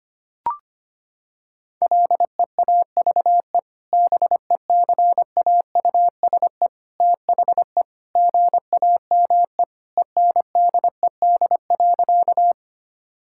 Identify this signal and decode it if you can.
Morse code, heard as a keyed tone, about 25 wpm, 700 Hz. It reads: LEAVE BECAUSE THE GAME ENDED.
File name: practice / 25wpm / Sentences with 2-Word Phrases - 25wpm